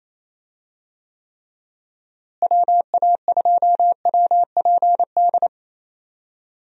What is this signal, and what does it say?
Morse code, heard as a keyed tone, about 28 wpm, 700 Hz. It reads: WA2WPD
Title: USA Call Signs - 6 Characters - Repeated 2x - 28wpm